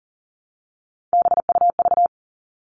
Morse code, heard as a keyed tone, about 40 wpm, 700 Hz. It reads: BUV
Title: All Cut Numbers - 3-Digit 40wpm